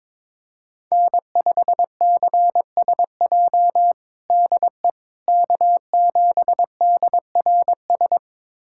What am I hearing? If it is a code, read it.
Morse code, heard as a keyed tone, about 22 wpm, 700 Hz. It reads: N5CSJ DE K7DRS
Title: USA Callsign DE Callsign - 5 Characters - 22wpm